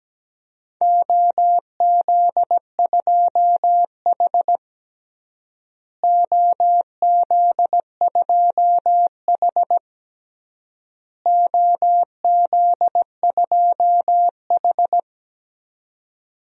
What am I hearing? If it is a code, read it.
Morse code, heard as a keyed tone, about 17 wpm, 700 Hz. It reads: OZ2H OZ2H OZ2H
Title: DXCC-100 Call Signs - 4 Characters - Repeated 3x - 17wpm